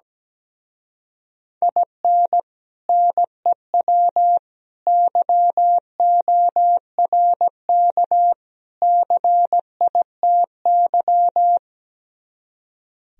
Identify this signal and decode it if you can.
Morse code, heard as a keyed tone, about 17 wpm, 700 Hz. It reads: IN NEW YORK CITY
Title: Speed-Racing - Sets of 4 Words 17wpm